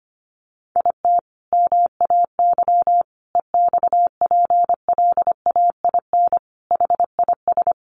Morse code, heard as a keyed tone, about 25 wpm, 700 Hz. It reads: IT MAY EXPLAIN HIS
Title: Sentences from Top 1300 Words 25wpm